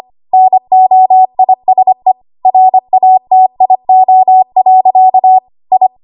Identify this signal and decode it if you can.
Morse code, heard as a keyed tone, about 25 wpm, 750 Hz. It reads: NOISE RATIO. I
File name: Archive / 25 WPM / 25WPM